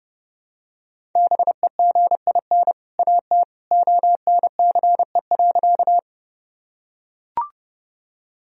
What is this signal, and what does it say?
Morse code, heard as a keyed tone, about 29 wpm, 700 Hz. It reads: BEGIN AT ONCE.
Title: Sentences from Top 1000 Words - Review Part 1 - 30wpm